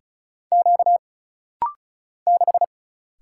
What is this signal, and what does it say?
Morse code, heard as a keyed tone, about 36 wpm, 700 Hz. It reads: Q 6
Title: Single Letter-Number - Rapid-Fire 35wpm